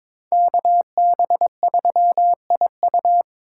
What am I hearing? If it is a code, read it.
Morse code, heard as a keyed tone, about 26 wpm, 700 Hz. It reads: KB3IU
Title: Speed Racing - USA Call Signs 22wpm